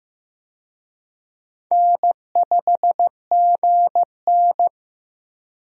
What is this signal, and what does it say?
Morse code, heard as a keyed tone, about 16 wpm, 700 Hz. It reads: N5GN
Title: USA Call Signs - 4 Characters - Repeated 3x - 15wpm